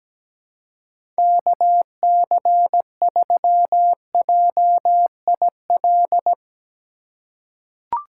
Morse code, heard as a keyed tone, about 17 wpm, 700 Hz. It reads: KC3JIL
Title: USA Call Signs - 6 Characters - 4x Character Spacing - 17wpm